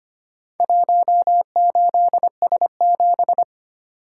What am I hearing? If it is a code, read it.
Morse code, heard as a keyed tone, about 25 wpm, 700 Hz. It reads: 18S7
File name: practice / 25wpm / Four Letter-Number Combinations - 25wpm